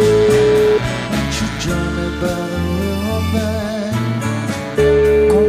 Ringback tone (Europe, 425 Hz), with noise.